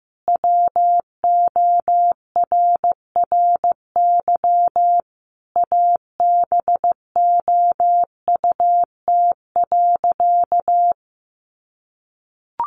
Morse code, heard as a keyed tone, about 15 wpm, 700 Hz. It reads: WORRY ABOUT.
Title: Sentences from Top 1100 Words 15wpm